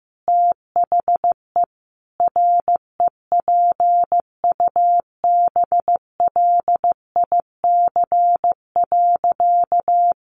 Morse code, heard as a keyed tone, about 15 wpm, 700 Hz. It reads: THE REPUBLIC.
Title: Sentences from Top 1500 Words 15wpm